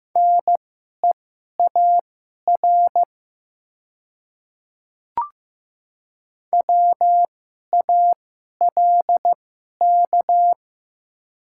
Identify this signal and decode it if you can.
Morse code, heard as a keyed tone, about 15 wpm, spaced as for 11 wpm, 700 Hz. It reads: NEAR WALK
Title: 2x Character Spacing - Top 300 Words - 15wpm